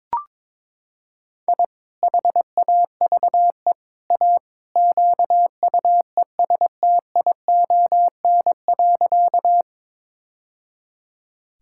Morse code, heard as a keyed tone, about 22 wpm, 700 Hz. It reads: I HAVE A QUESTION.
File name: practice / 22wpm / Sentences with 3-Word Phrases - 22wpm